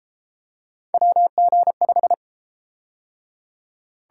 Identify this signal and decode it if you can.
Morse code, heard as a keyed tone, about 31 wpm, 700 Hz. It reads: WG5